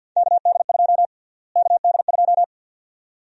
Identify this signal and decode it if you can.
Morse code, heard as a keyed tone, about 53 wpm, 700 Hz. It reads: KD2 KD2
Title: USA Call Sign Prefixes - Repeated 2x - 50wpm